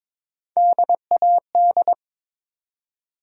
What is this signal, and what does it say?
Morse code, heard as a keyed tone, about 22 wpm, 700 Hz. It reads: DAD
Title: USA Call Sign Postfixes - 3 Letters 22wpm